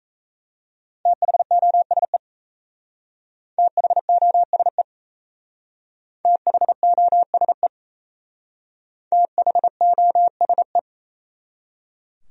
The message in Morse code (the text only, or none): THOSE THOSE THOSE THOSE